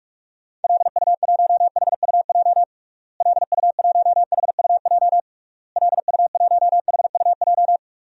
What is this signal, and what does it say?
Morse code, heard as a keyed tone, about 46 wpm, 700 Hz. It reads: LU1HUJ LU1HUJ LU1HUJ
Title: DXCC-100 Call Signs - 6 Characters - Repeated 3x - 45wpm